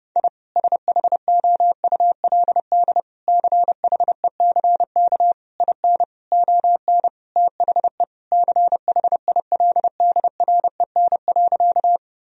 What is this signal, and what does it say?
Morse code, heard as a keyed tone, about 32 wpm, 700 Hz. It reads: I SHOULD CHECK IN ON THE CHILDREN.
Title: Sentences from Top 1000 Words - Review Part 3 - 30wpm